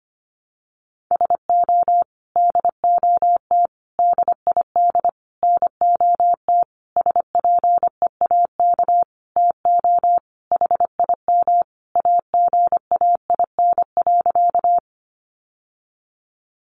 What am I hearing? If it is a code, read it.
Morse code, heard as a keyed tone, about 25 wpm, 700 Hz. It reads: SO DOT DID NOT SPEAK TO HIM AGAIN.
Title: Sentences from Top 1500 Words 25wpm